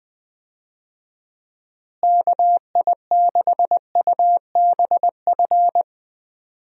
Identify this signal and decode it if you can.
Morse code, heard as a keyed tone, about 20 wpm, 700 Hz. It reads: KI6UBF
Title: USA Call Signs - 6 Characters - Repeated 2x - 20wpm